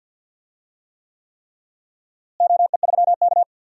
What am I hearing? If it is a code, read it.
Morse code, heard as a keyed tone, about 52 wpm, 700 Hz. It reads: OE3K
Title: DXCC-100 Call Signs - 4 Characters - Repeated 3x - 50wpm